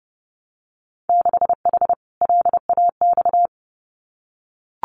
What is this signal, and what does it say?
Morse code, heard as a keyed tone, about 30 wpm, 700 Hz. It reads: 6H LAX
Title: Winter Field Day Exchanges Encore 30wpm